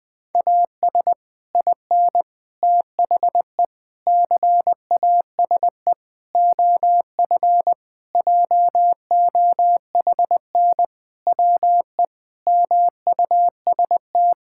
Morse code, heard as a keyed tone, about 20 wpm, 700 Hz. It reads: AS IN THE CASE OF JOHN WE MUST
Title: Sentences with 5-Word Phrases - 20wpm